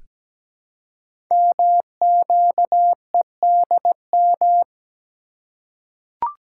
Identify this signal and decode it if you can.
Morse code, heard as a keyed tone, about 17 wpm, 700 Hz. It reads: MQEDM
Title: Code Groups - 5 Letter:Numbers - 17wpm